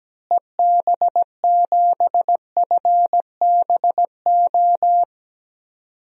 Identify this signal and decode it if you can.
Morse code, heard as a keyed tone, about 17 wpm, 700 Hz. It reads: EB7FBO
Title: DXCC-100 Call Signs - 6 Characters - Repeated 2x - 17wpm